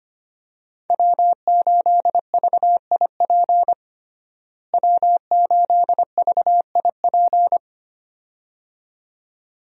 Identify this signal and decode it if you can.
Morse code, heard as a keyed tone, about 25 wpm, 700 Hz. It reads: W8VIP W8VIP
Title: USA Call Signs - 5 Characters - Repeated 2x - 25wpm